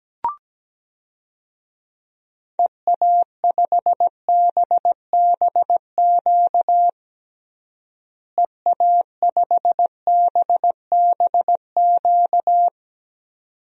Morse code, heard as a keyed tone, about 17 wpm, 700 Hz. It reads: EA5BBQ EA5BBQ
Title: DXCC-100 Call Signs - 6 Characters - Repeated 2x - 17wpm